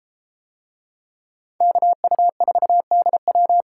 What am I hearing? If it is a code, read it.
Morse code, heard as a keyed tone, about 31 wpm, 700 Hz. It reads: KU4DW